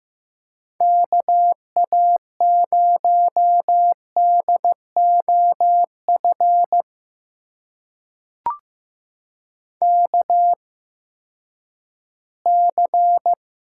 Morse code, heard as a keyed tone, about 15 wpm, spaced as for 10 wpm, 700 Hz. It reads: KA0DOF KC
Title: USA Call Signs - 6 Characters - 8x Character Spacing - 15wpm